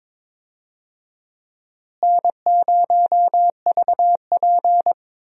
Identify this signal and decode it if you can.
Morse code, heard as a keyed tone, about 22 wpm, 700 Hz. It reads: N0VP
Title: USA Call Signs - 4 Characters - Repeated 2x - 22wpm